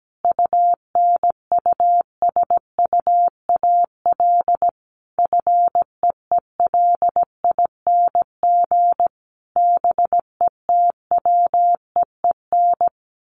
Morse code, heard as a keyed tone, about 17 wpm, 700 Hz. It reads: UNUSUAL FEELING BETWEEN
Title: Sentences from Top 1400 Words 17wpm